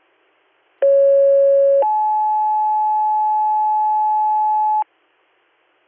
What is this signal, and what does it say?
Two-tone page, heard as a one-second tone, then a three-second tone, about 550 then 860 Hz, with noise. It is heard from 0.8 to 4.8 s.